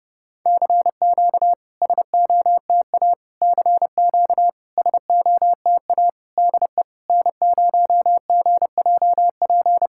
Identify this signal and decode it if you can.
Morse code, heard as a keyed tone, about 30 wpm, 690 Hz. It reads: CQ SOTA CQ SOTA DE N0GJP
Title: SOTA QSO Practice 30wpm